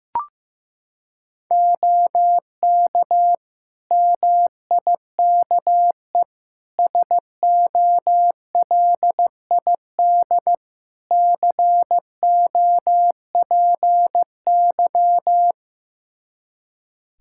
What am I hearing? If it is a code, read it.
Morse code, heard as a keyed tone, about 15 wpm, 700 Hz. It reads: OK MIKE SOLID COPY
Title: Practice QSO - Exchange 3 - More Info - 15wpm